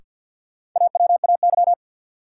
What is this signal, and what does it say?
Morse code, heard as a keyed tone, about 53 wpm, 700 Hz. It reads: AWAY
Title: Sets of 2 Words - New Word Drill - 50wpm